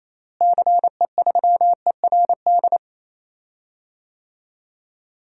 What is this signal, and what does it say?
Morse code, heard as a keyed tone, about 27 wpm, 700 Hz. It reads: CE3ERD
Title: DXCC-100 Call Signs - 6 Characters - Repeated 2x - 28wpm